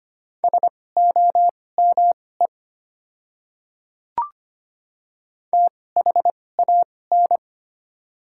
Morse code, heard as a keyed tone, about 25 wpm, spaced as for 18 wpm, 700 Hz. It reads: SOME THAN
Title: 2x Character Spacing - Top 200 Words - 25wpm